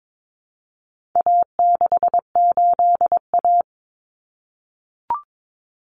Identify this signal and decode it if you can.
Morse code, heard as a keyed tone, about 22 wpm, 700 Hz. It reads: A68A
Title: Common Cut Numbers - 4-Digit 22wpm